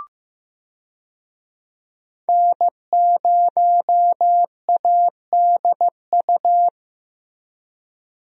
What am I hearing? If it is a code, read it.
Morse code, heard as a keyed tone, about 15 wpm, 700 Hz. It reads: N0ADU
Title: USA Call Signs - 5 Characters - Repeated 2x - 15wpm